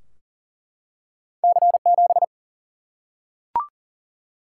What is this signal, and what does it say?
Morse code, heard as a keyed tone, about 40 wpm, 700 Hz. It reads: C7
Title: Single-Letter Single-Number 40wpm